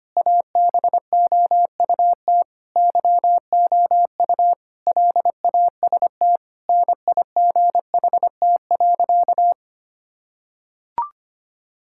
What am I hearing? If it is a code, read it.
Morse code, heard as a keyed tone, about 25 wpm, 700 Hz. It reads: ABOUT YOU LAST NIGHT.
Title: Sentences from Top 1600 Words 25wpm